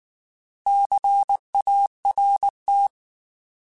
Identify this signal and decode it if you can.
Morse code, heard as a keyed tone, about 19 wpm, 800 Hz. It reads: CART
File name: Alphabetical / Alphabetical mp3s / Cs @18